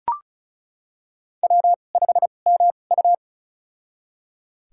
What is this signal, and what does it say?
Morse code, heard as a keyed tone, about 34 wpm, spaced as for 25 wpm, 700 Hz. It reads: W5MU
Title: USA Call Signs - 4 Characters - 2x Character Spacing - 35wpm